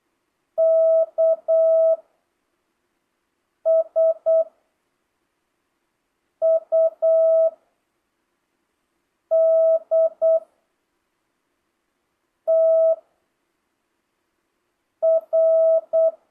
Morse code, heard as a keyed tone, about 8 wpm, 650 Hz. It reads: K S U D T R
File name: jh8cbh / 2025cw